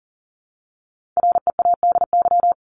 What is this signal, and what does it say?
Morse code, heard as a keyed tone, about 40 wpm, 700 Hz. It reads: READY
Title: Top 500 Words 40wpm